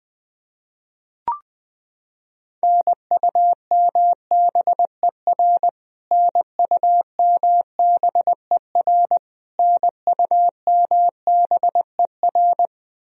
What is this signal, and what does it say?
Morse code, heard as a keyed tone, about 20 wpm, 700 Hz. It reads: NUMBER NUMBER NUMBER
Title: Top 100 Words Repeated 20wpm